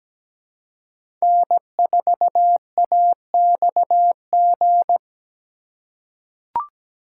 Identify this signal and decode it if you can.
Morse code, heard as a keyed tone, about 17 wpm, 700 Hz. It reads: N4AXG
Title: USA Call Signs - 5 Characters - 4x Character Spacing - 17wpm